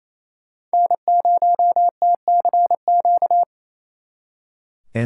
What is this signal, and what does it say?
Morse code, heard as a keyed tone, about 36 wpm, 700 Hz. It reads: N0TCQ